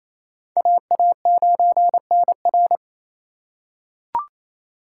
Morse code, heard as a keyed tone, about 28 wpm, 700 Hz. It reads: AA9NR
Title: USA Call Signs - 3x Character Spacing - 28wpm